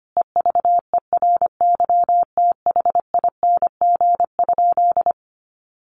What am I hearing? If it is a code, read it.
Morse code, heard as a keyed tone, about 25 wpm, 700 Hz. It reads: EVERYTHING?
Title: Sentences from Top 1100 Words 25wpm